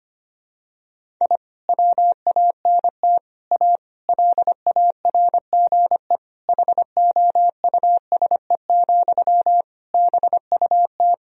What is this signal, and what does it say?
Morse code, heard as a keyed tone, about 25 wpm, 700 Hz. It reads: I WANT A LARGE HOUSE, BUT
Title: Sentences from Top 1000 Words - Review Part 1 - 25wpm